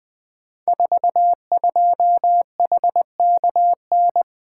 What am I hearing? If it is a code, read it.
Morse code, heard as a keyed tone, about 20 wpm, 700 Hz. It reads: 42HKN
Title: Code Groups - 5 Letter:Numbers - 20wpm